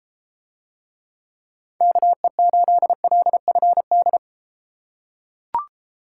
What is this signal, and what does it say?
Morse code, heard as a keyed tone, about 33 wpm, 700 Hz. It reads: KE8LFD